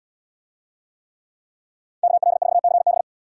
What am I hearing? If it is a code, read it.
Morse code, heard as a keyed tone, about 78 wpm, 700 Hz. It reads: KL4CB